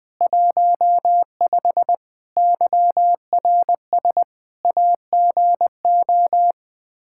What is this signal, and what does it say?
Morse code, heard as a keyed tone, about 21 wpm, 700 Hz. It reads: 15 YRS AGO
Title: Practice QSO - Exchange 3 - More Info - 20wpm